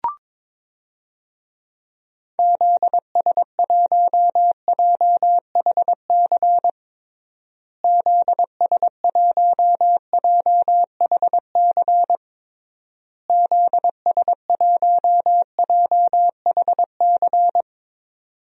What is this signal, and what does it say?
Morse code, heard as a keyed tone, about 22 wpm, 700 Hz. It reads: ZS1JHC ZS1JHC ZS1JHC
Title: DXCC-100 Call Signs - 6 Characters - Repeated 3x - 22wpm